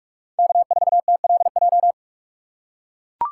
Morse code, heard as a keyed tone, about 41 wpm, 700 Hz. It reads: K4TLJ